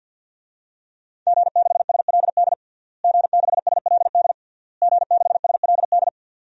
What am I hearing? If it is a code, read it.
Morse code, heard as a keyed tone, about 50 wpm, 700 Hz. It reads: G6SLD G6SLD G6SLD
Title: DXCC-100 Call Signs - 5 Characters - Repeated 3x - 50wpm